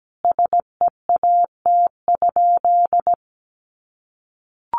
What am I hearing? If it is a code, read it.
Morse code, heard as a keyed tone, about 17 wpm, 700 Hz. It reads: SEAT?
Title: Sentences from Top 1500 Words 17wpm